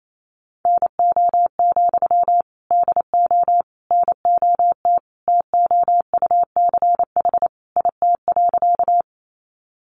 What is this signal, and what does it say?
Morse code, heard as a keyed tone, about 28 wpm, 700 Hz. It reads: NO, DO NOT TOUCH IT.
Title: Sentences from Top 1200 Words 28wpm